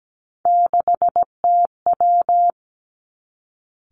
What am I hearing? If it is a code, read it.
Morse code, heard as a keyed tone, about 17 wpm, 700 Hz. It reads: 6TW
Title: Three Challenging Letter-Number Combinations 17wpm